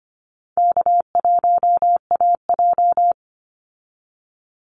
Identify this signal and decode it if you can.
Morse code, heard as a keyed tone, about 25 wpm, 700 Hz. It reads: K1AJ
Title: Easy Call Signs Repeated 3x - 25wpm